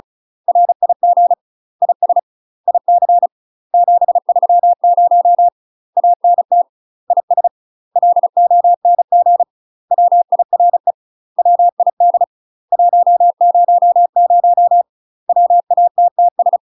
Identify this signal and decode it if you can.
Morse code, heard as a keyed tone, about 36 wpm, 700 Hz. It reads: RIG IS IC 730 ANT IS LONG WIRE WID 100 WATTS
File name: UK MASTER QSOs TRAINING / 35WPM / UK MASTER QSOs 0028